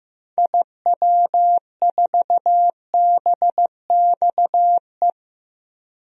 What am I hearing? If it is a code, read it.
Morse code, heard as a keyed tone, about 15 wpm, 700 Hz. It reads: IW4BXE
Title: DXCC-100 Call Signs - 6 Characters - 15wpm